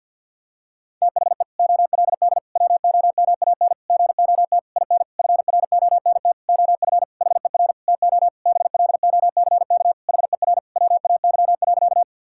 Morse code, heard as a keyed tone, about 52 wpm, 700 Hz. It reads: THE OLD WOMAN GOT IN FRONT OF HER TO BLOCK HER WAY.
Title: Sentences from Top 1600 Words 50wpm